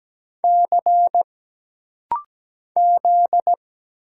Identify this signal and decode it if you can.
Morse code, heard as a keyed tone, about 17 wpm, 700 Hz. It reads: C Z
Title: Single Letter-Number Alt - Rapid-Fire 17wpm